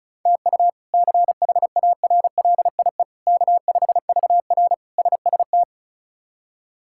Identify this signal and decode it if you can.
Morse code, heard as a keyed tone, about 36 wpm, 700 Hz. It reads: TU CHARLIE K5VR SST
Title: K1USN SST QSO Practice - 35wpm